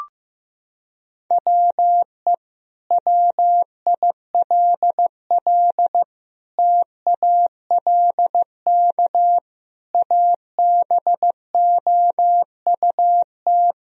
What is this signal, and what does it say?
Morse code, heard as a keyed tone, about 15 wpm, 700 Hz. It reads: WE WILL TALK ABOUT
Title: Sentences from Top 1000 Words - Review Part 4 - 15wpm